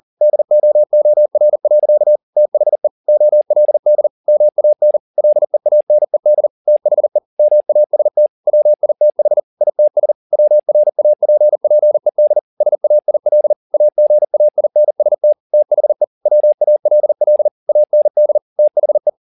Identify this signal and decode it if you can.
Morse code, heard as a keyed tone, about 40 wpm, 600 Hz. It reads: DOOR. THE OLD MAN LEANED THE MAST WITH ITS WRAPPED SAIL AGAINST THE WALL AND THE